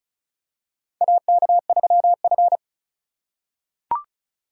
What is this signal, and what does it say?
Morse code, heard as a keyed tone, about 36 wpm, 700 Hz. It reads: AK3F
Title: USA Call Signs - 4 Characters - 3x Character Spacing - 35wpm